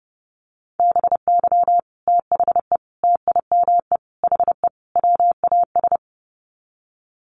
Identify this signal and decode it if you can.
Morse code, heard as a keyed tone, about 30 wpm, 700 Hz. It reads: BY THE TIME HE WAS
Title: Sets of 5 Words Encore 30wpm